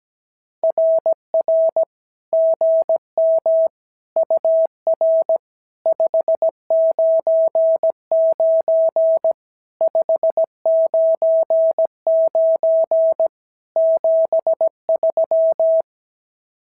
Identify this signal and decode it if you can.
Morse code, heard as a keyed tone, about 17 wpm, 650 Hz. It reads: RR GM UR 599 599 73
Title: SOTA QSO Practice 17wpm